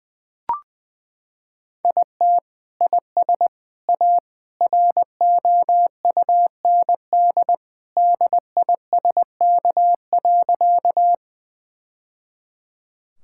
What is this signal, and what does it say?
Morse code, heard as a keyed tone, about 20 wpm, 700 Hz. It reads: IT IS A ROUND DISK.
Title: Sentences from Top 1000 Words - Review Part 6 - 20wpm